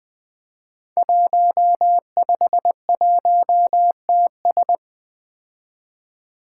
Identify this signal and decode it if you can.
Morse code, heard as a keyed tone, about 20 wpm, 700 Hz. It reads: 151TS
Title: Five Challenging Letter-Number Combinations 20wpm